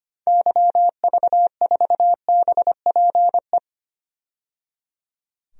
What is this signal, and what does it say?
Morse code, heard as a keyed tone, about 25 wpm, 700 Hz. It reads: YV4BPE
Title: DXCC-100 Call Signs - 6 Characters - 25wpm